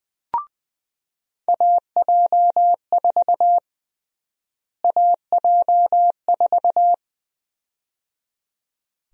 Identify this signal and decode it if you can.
Morse code, heard as a keyed tone, about 20 wpm, 700 Hz. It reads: AJ4 AJ4
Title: USA Call Sign Prefixes - Repeated 2x - 20wpm